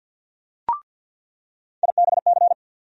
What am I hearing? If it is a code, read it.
Morse code, heard as a keyed tone, about 47 wpm, 700 Hz. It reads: IBC
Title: USA Call Sign Postfixes - 3 Letters 50wpm